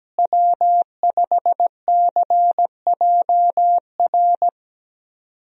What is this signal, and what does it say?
Morse code, heard as a keyed tone, about 17 wpm, 700 Hz. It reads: W5CJR
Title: USA Call Signs - 5 Characters - Repeated 3x - 17wpm